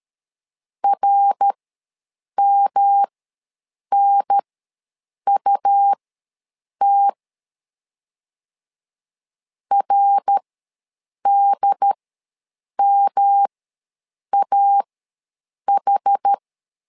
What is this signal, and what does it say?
Morse code, heard as a keyed tone, about 13 wpm, spaced as for 7 wpm, 800 Hz. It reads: RMNUT RDMAH
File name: LEZIONI / 3 MWRO / 3 MWR ws 12wpm 1809